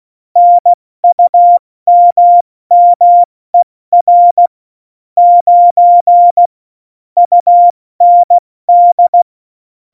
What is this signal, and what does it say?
Morse code, heard as a keyed tone, about 16 wpm, 700 Hz. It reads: NUMMER 9 UND
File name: Morsekurs / Witze / Witz 18 August 2020, 70 F80